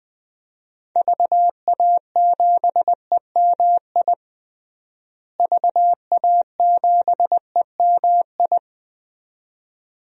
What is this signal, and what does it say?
Morse code, heard as a keyed tone, about 20 wpm, 700 Hz. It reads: VA7EMI VA7EMI
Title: DXCC-100 Call Signs - 6 Characters - Repeated 2x - 20wpm